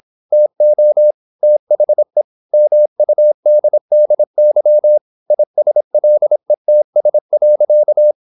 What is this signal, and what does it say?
Morse code, heard as a keyed tone, about 26 wpm, 600 Hz. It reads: TO THE MUDDY ISLETS.
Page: Never 26wpm | CW med Gnister